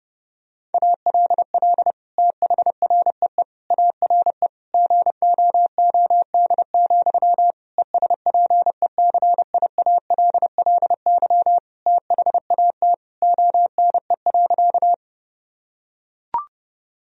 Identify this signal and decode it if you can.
Morse code, heard as a keyed tone, about 31 wpm, 700 Hz. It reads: ALL THREE ARE GOOD, ESPECIALLY THAT ONE.
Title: Sentences from Top 1000 Words - Review Part 4 - 30wpm